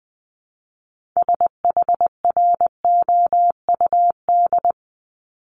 Morse code, heard as a keyed tone, about 20 wpm, 700 Hz. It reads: SHROUD
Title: Multi-letter Phonemes 20wpm